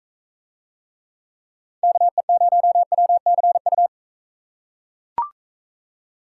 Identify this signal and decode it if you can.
Morse code, heard as a keyed tone, about 40 wpm, 700 Hz. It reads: KE0WCU